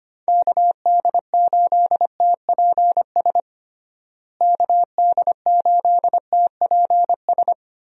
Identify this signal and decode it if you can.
Morse code, heard as a keyed tone, about 25 wpm, 700 Hz. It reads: KD8TPS KD8TPS
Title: USA Call Signs - 6 Characters - Repeated 2x - 25wpm